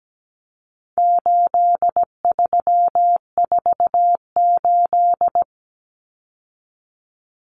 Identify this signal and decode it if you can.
Morse code, heard as a keyed tone, about 17 wpm, 700 Hz. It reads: 8348
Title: Common Cut Numbers - 4-Digit 17wpm